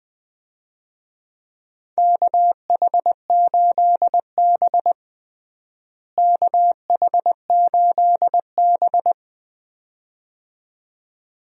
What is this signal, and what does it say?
Morse code, heard as a keyed tone, about 20 wpm, 700 Hz. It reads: KH8B KH8B
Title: USA Call Signs - 4 Characters - Repeated 2x - 20wpm